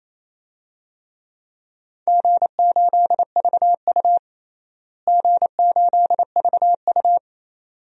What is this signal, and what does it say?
Morse code, heard as a keyed tone, about 28 wpm, 700 Hz. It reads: G8VU G8VU
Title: DXCC-100 Call Signs - 4 Characters - Repeated 2x - 28wpm